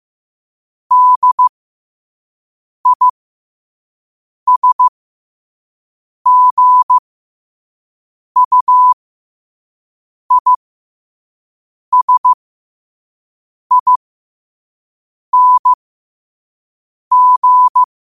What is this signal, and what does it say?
Morse code, heard as a keyed tone, about 15 wpm, spaced as for 5 wpm, 1000 Hz. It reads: DISGUISING
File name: qotdfiles / 5wpm